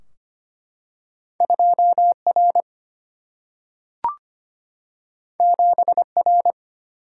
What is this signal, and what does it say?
Morse code, heard as a keyed tone, about 25 wpm, 700 Hz. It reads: 2R 7R
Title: Single-Letter Single-Number 25wpm